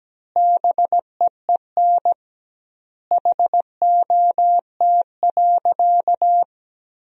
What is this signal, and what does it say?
Morse code, heard as a keyed tone, about 17 wpm, 700 Hz. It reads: BEEN HOT.
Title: 2x Word-Spacing - Sentences from Top 500 Words - 17wpm